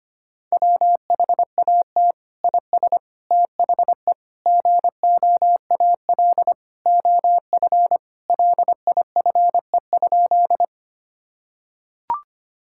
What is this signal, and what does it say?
Morse code, heard as a keyed tone, about 25 wpm, 700 Hz. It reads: WHAT IS THE GOAL OF LIFE?
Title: Sentences from Top 1000 Words - Review Part 5 - 25wpm